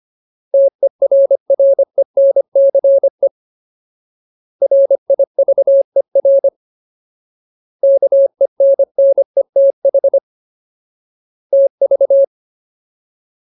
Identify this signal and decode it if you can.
Morse code, heard as a keyed tone, about 25 wpm, 550 Hz. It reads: TERRENCE RIVER KENNETH TV
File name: Ultimate Word List / 25WPM Ultimate